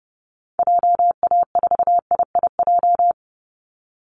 Morse code, heard as a keyed tone, about 30 wpm, 700 Hz. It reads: JA4IIJ
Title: Call Signs Repeated 2x - 30wpm